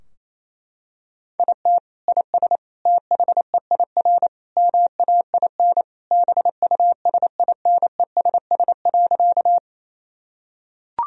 Morse code, heard as a keyed tone, about 28 wpm, 700 Hz. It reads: IT IS THEIR MAIN BUSINESS.
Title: Sentences from Top 1000 Words - Review Part 2 - 28wpm